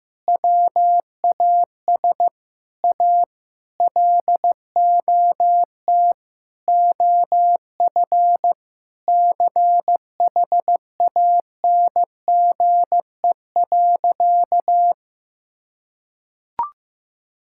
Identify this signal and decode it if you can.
Morse code, heard as a keyed tone, about 16 wpm, 700 Hz. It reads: WAS A LOT OF CHANGE.
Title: Sentences from Top 1000 Words - Review Part 5 - 15wpm